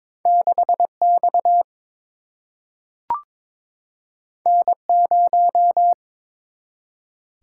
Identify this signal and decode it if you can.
Morse code, heard as a keyed tone, about 22 wpm, 700 Hz. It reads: 6 X N 0
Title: Single-Letter Single-Number 22wpm